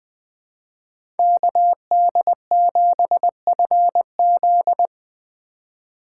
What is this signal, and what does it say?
Morse code, heard as a keyed tone, about 20 wpm, 700 Hz. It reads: KD7FZ
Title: USA Call Signs - 5 Characters - 4x Character Spacing - 20wpm